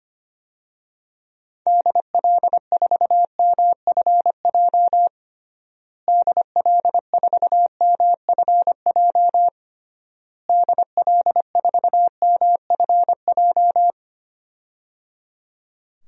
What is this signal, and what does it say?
Morse code, heard as a keyed tone, about 25 wpm, 700 Hz. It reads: DL4MFJ DL4MFJ DL4MFJ
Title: DXCC-100 Call Signs - 6 Characters - Repeated 3x - 25wpm